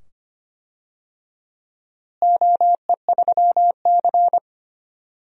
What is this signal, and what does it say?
Morse code, heard as a keyed tone, about 25 wpm, 700 Hz. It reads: OE3C
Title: DXCC-100 Call Signs - 4 Characters - Repeated 2x - 25wpm